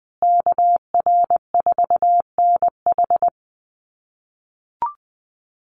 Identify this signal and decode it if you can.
Morse code, heard as a keyed tone, about 20 wpm, 700 Hz. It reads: KR4NH
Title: USA Call Signs Repeated 3x 20wpm